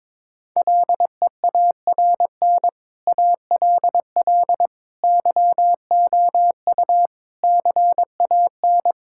Morse code, heard as a keyed tone, about 22 wpm, 700 Hz. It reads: LEARN ALL YOU CAN
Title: Sentences from Top 1000 Words - Review Part 5 - 22wpm